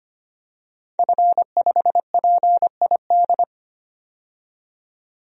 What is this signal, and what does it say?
Morse code, heard as a keyed tone, about 25 wpm, 700 Hz. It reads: F5PID
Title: DXCC-100 Call Signs - 5 Characters - 25wpm